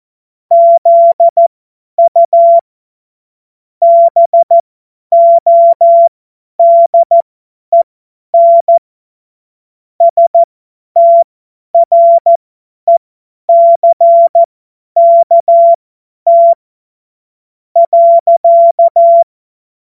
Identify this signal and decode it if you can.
Morse code, heard as a keyed tone, about 14 wpm, spaced as for 9 wpm, 680 Hz. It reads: ZU BODEN STRECKT.
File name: Morsekurs / Witze / Witz 13 Februar 2024 50 F70